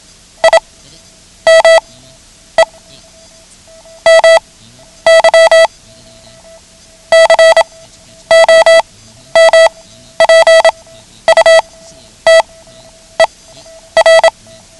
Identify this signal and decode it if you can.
Morse code, heard as a keyed tone, about 27 wpm, spaced as for 11 wpm, 650 Hz. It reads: I ME MY COMPUTER
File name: ham radio / CodeQuick / Code Quick 09